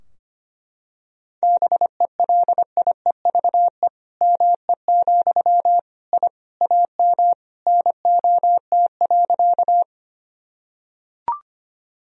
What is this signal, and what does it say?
Morse code, heard as a keyed tone, about 25 wpm, 700 Hz. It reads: BELIEVE ME, I AM NOT.
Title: Sentences from Top 1000 Words - Review Part 2 - 25wpm